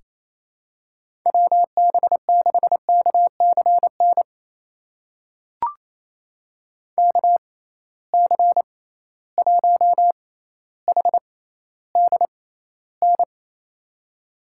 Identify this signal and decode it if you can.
Morse code, heard as a keyed tone, about 27 wpm, spaced as for 14 wpm, 700 Hz. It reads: WB6KCN KC1HDN
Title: USA Call Signs - 6 Characters - 6x Character Spacing - 28wpm